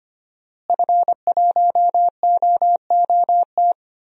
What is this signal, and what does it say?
Morse code, heard as a keyed tone, about 25 wpm, 700 Hz. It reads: F1OOT
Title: DXCC-100 Call Signs - 5 Characters - Repeated 2x - 25wpm